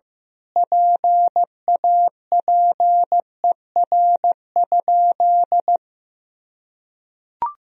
Morse code, heard as a keyed tone, about 15 wpm, 700 Hz. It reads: PAPER?
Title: Sentences from Top 1000 Words - Review Part 2 - 15wpm